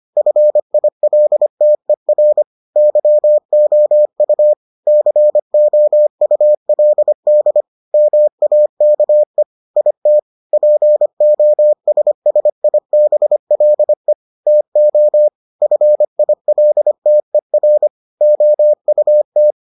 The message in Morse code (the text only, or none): FILTER YOU COULD MAKE IT POSSIBLE TO FILTER OUT